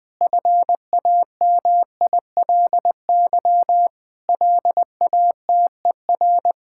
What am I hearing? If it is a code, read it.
Morse code, heard as a keyed tone, about 20 wpm, 700 Hz. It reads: FAMILY LATER
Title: Sentences from Top 1600 Words 20wpm